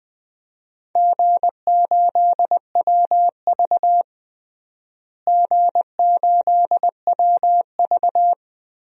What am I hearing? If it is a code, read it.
Morse code, heard as a keyed tone, about 21 wpm, 700 Hz. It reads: G8WV G8WV
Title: DXCC-100 Call Signs - 4 Characters - Repeated 3x - 20wpm